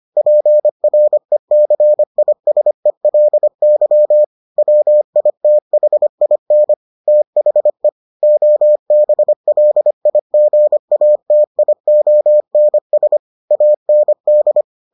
Morse code, heard as a keyed tone, about 25 wpm, 600 Hz. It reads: PRECISELY WITHIN THE OBLIGATIONS AND